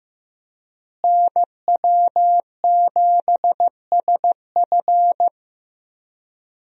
Morse code, heard as a keyed tone, about 15 wpm, 700 Hz. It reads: NW7SF
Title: USA Call Signs - 5 Characters - Repeated 3x - 15wpm